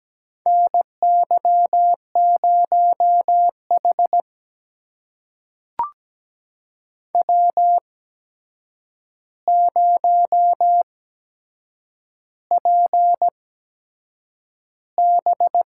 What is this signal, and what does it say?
Morse code, heard as a keyed tone, about 17 wpm, spaced as for 7 wpm, 700 Hz. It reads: NY0H W0PB
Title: USA Call Signs - 4 Characters - 8x Character Spacing - 17wpm